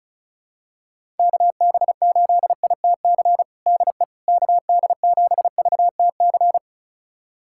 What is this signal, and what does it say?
Morse code, heard as a keyed tone, about 36 wpm, 700 Hz. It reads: KB8ITC DE KD7VTC
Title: USA Callsign DE Callsign - 6 Characters - 35wpm